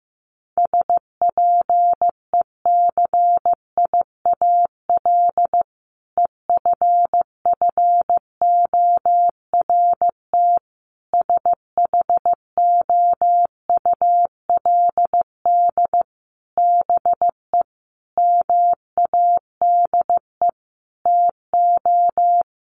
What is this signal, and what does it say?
Morse code, heard as a keyed tone, about 14 wpm, 700 Hz. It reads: SPECIAL EFFORT SHOULD BE MADE TO
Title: Sentences from Top 1200 Words 15wpm